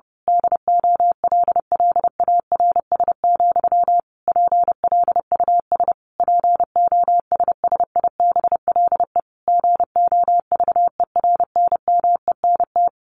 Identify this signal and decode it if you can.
Morse code, heard as a keyed tone, about 30 wpm, 700 Hz. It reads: DOLLARS, PLUS POSSIBLE GOVERNMENT
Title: A Sound Of Thunder 30wpm